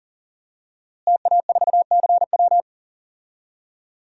Morse code, heard as a keyed tone, about 40 wpm, 700 Hz. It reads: TA4CW
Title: DXCC-100 Call Signs - 5 Characters - Repeated 2x - 40wpm